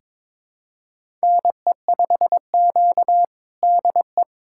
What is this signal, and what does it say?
Morse code, heard as a keyed tone, about 22 wpm, 700 Hz. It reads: NE5Q DE
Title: USA Callsign DE Callsign - 4 Characters - 22wpm